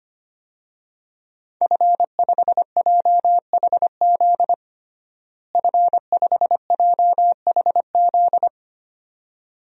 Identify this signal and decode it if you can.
Morse code, heard as a keyed tone, about 25 wpm, 700 Hz. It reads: F5JHZ F5JHZ
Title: DXCC-100 Call Signs - 5 Characters - Repeated 2x - 25wpm